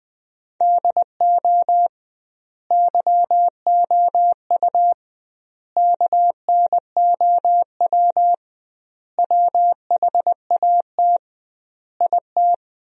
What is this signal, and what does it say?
Morse code, heard as a keyed tone, about 20 wpm, 700 Hz. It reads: DO YOU KNOW WHAT IT
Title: 2x Word-Spacing - Sentences from Top 100 Words - 20wpm